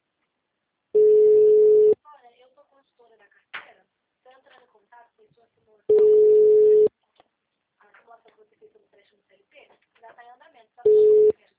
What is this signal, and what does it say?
Ringback tone (Europe, 425 Hz), with speech.
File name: out- -2003- - - .16550